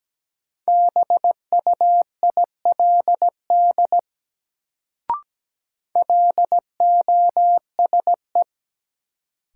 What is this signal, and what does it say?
Morse code, heard as a keyed tone, about 17 wpm, 700 Hz. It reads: BUILD LOSE
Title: Sentences from Top 1100 Words - New Word Drill Alt - 17wpm